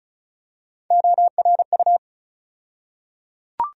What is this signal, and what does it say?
Morse code, heard as a keyed tone, about 34 wpm, 700 Hz. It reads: ORU
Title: USA Call Sign Postfixes - 3 Letters 35wpm